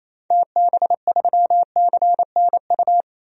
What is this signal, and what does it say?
Morse code, heard as a keyed tone, about 28 wpm, 700 Hz. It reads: TB3CNU
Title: DXCC-100 Call Signs - 6 Characters - 28wpm